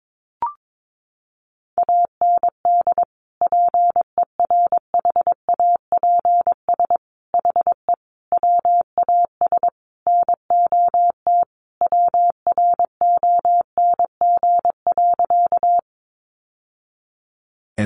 Morse code, heard as a keyed tone, about 22 wpm, 700 Hz. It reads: AND PERHAPS HE WAS NOT WRONG.
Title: Sentences from Top 900 Words 22wpm